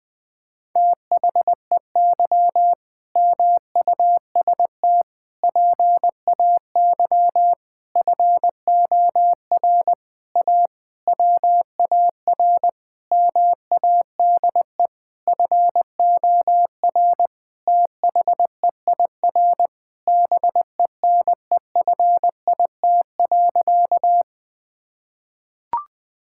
Morse code, heard as a keyed tone, about 20 wpm, 700 Hz. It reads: THEY MUST PAY FOR A WAR MADE FOR THEIR BENEFIT.
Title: Sentences from Top 1000 Words - Review Part 6 - 20wpm